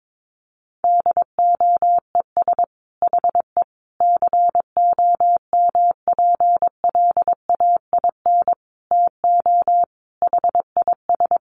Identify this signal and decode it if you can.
Morse code, heard as a keyed tone, about 22 wpm, 700 Hz. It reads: DOES HE COMPLAIN TO HIS
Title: Sentences from Top 1300 Words 22wpm